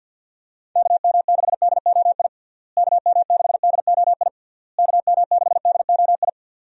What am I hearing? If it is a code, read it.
Morse code, heard as a keyed tone, about 50 wpm, 700 Hz. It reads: KM6DOI KM6DOI KM6DOI
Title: USA Call Signs - 6 Characters - Repeated 3x - 50wpm